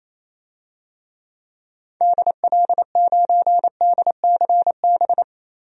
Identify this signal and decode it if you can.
Morse code, heard as a keyed tone, about 28 wpm, 700 Hz. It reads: DL9DCB
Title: DXCC-100 Call Signs - 6 Characters - 28wpm